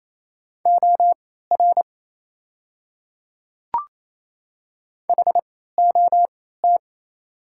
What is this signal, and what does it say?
Morse code, heard as a keyed tone, about 28 wpm, spaced as for 16 wpm, 690 Hz. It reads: OR HOT
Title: 3x Character Spacing - Top 100 Words - 28wpm